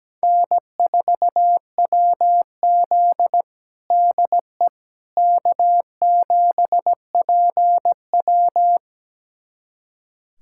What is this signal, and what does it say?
Morse code, heard as a keyed tone, about 17 wpm, 700 Hz. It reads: N4WZ DE K7PW
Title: USA Callsign DE Callsign - 4 Characters - 17wpm